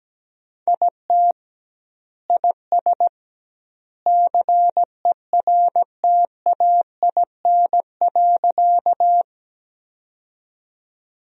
Morse code, heard as a keyed tone, about 17 wpm, 700 Hz. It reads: IT IS CERTAIN.
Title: 2x Word-Spacing - Sentences from Top 500 Words - 17wpm